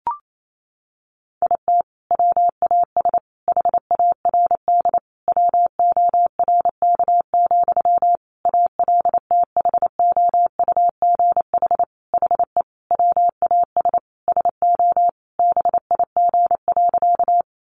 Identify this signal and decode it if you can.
Morse code, heard as a keyed tone, about 28 wpm, 700 Hz. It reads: IT WAS HARD WORK, ALTHOUGH HE WAS SO BIG.
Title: Sentences from Top 600 Words 28wpm